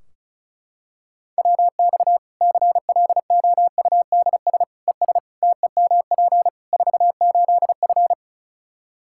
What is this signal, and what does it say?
Morse code, heard as a keyed tone, about 36 wpm, 700 Hz. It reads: WX CLOUDS ES TEMP 48F
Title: Practice QSO - Exchange 2 - Info - 35wpm